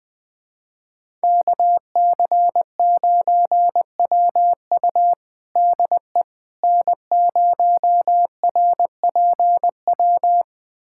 Morse code, heard as a keyed tone, about 21 wpm, 700 Hz. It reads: KC9WU DE N0RPW
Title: USA Callsign DE Callsign - 5 Characters - 20wpm